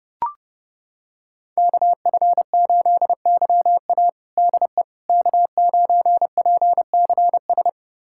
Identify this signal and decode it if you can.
Morse code, heard as a keyed tone, about 32 wpm, 700 Hz. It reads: KF8YA DE K9PCS
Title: USA Callsign DE Callsign - 5 Characters - 30wpm